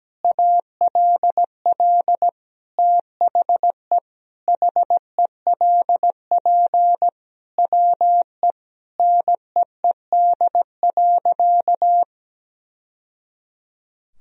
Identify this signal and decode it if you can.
Morse code, heard as a keyed tone, about 17 wpm, 700 Hz. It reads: ALL THE HELP WE NEED.
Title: Sentences from Top 1000 Words - Review Part 1 - 17wpm